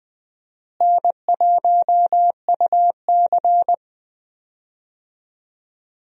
Morse code, heard as a keyed tone, about 20 wpm, 700 Hz. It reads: N1UC
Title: USA Call Signs - 4 Characters - Repeated 2x - 20wpm